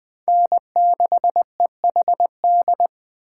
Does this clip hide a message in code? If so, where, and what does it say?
Yes: Morse code from 0.3 to 2.9 s, heard as a keyed tone, about 20 wpm, 700 Hz. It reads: N6EHD